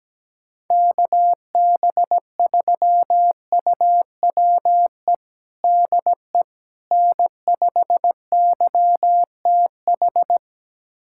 Morse code, heard as a keyed tone, about 17 wpm, 700 Hz. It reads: KB3UWE DE N5YTH
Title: USA Callsign DE Callsign 17wpm